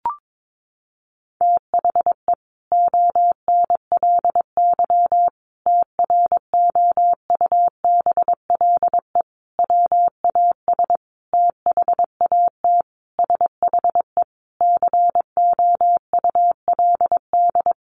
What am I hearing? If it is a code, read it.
Morse code, heard as a keyed tone, about 22 wpm, 700 Hz. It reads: THE ONLY TROUBLE WAS THAT SHE COULD
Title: Sentences from Top 1200 Words 22wpm